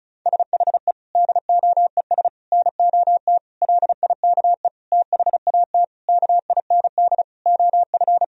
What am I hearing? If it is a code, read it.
Morse code, heard as a keyed tone, about 34 wpm, 700 Hz. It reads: SHE DOES NOT LIKE THAT KIND OF
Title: Sentences from Top 1000 Words - Review Part 4 - 35wpm